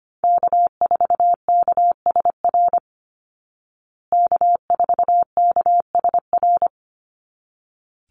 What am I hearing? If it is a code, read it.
Morse code, heard as a keyed tone, about 25 wpm, 700 Hz. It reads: K4KSR K4KSR
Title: Call Signs Repeated 3x - 25wpm